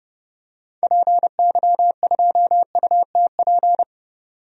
Morse code, heard as a keyed tone, about 29 wpm, 690 Hz. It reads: PY2UTP
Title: DXCC-100 Call Signs - 6 Characters - 30wpm